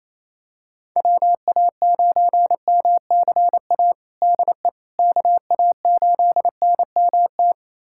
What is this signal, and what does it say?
Morse code, heard as a keyed tone, about 28 wpm, 700 Hz. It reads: WA9MCA DE KA8NMT
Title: USA Callsign DE Callsign - 6 Characters - 28wpm